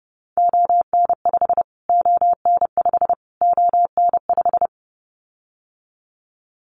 Morse code, heard as a keyed tone, about 30 wpm, 700 Hz. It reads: ON5 ON5 ON5
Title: Callsign Prefixes for Top 25 DXCC - Repeated 3x - 30wpm